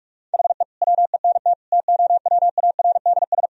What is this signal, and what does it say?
Morse code, heard as a keyed tone, about 41 wpm, 700 Hz. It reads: HE WENT TOWARDS
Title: Sentences from Top 1000 Words - Review Part 2 - 45wpm